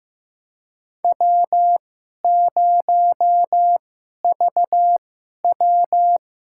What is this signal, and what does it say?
Morse code, heard as a keyed tone, about 15 wpm, spaced as for 11 wpm, 700 Hz. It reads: W0VW
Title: USA Call Signs - 4 Characters - 2x Character Spacing - 15wpm